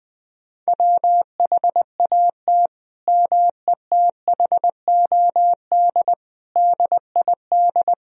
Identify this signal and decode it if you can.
Morse code, heard as a keyed tone, about 20 wpm, 700 Hz. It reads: WHAT METHOD DID
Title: Sentences from Top 1000 Words - Review Part 3 - 20wpm